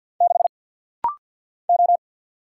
Morse code, heard as a keyed tone, about 50 wpm, 700 Hz. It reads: 6 O
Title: Single Letter-Number Alt - Rapid-Fire 50wpm